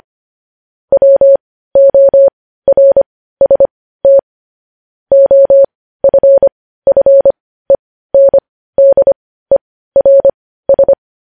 Morse code, heard as a keyed tone, about 25 wpm, spaced as for 15 wpm, 550 Hz. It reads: WORST OFFENDERS
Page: news headlines in morse code at 15 wpm